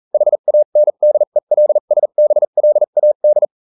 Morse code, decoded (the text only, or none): HANDELSBLAD